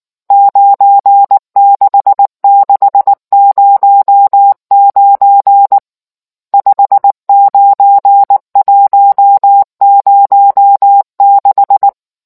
Morse code, heard as a keyed tone, about 19 wpm, 800 Hz. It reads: 96609 59106